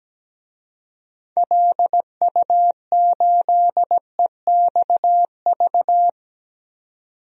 Morse code, heard as a keyed tone, about 17 wpm, 700 Hz. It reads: LU8EXV
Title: DXCC-100 Call Signs - 6 Characters - 17wpm